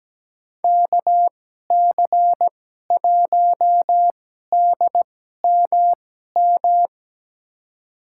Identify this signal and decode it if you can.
Morse code, heard as a keyed tone, about 17 wpm, spaced as for 12 wpm, 700 Hz. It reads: KC1DMM
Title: USA Call Signs - 6 Characters - 2x Character Spacing - 17wpm